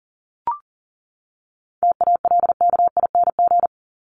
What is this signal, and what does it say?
Morse code, heard as a keyed tone, about 40 wpm, 700 Hz. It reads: TALKING
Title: ING Suffix - Top 100 Words with Base Words 40wpm